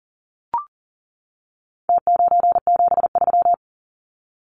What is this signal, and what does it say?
Morse code, heard as a keyed tone, about 40 wpm, 700 Hz. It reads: T973
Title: Numbers - Contest Style - 4-Digit 40wpm